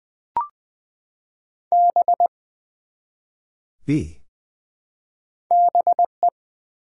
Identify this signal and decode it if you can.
Morse code, heard as a keyed tone, about 20 wpm, 700 Hz. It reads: B BE
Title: Build A Word - 1 Letter At A Time - 3 to 5-Letter Words - 20wpm